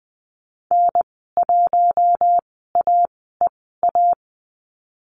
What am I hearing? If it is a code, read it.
Morse code, heard as a keyed tone, about 20 wpm, spaced as for 14 wpm, 700 Hz. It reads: N1AEA
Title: 042 Lesson 8g - Number 1 - Call Signs 20wpm